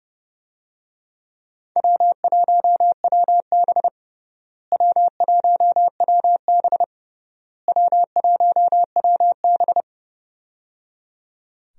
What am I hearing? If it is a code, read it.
Morse code, heard as a keyed tone, about 30 wpm, 700 Hz. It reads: W1WB W1WB W1WB
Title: USA Call Signs - 4 Characters - Repeated 3x - 30wpm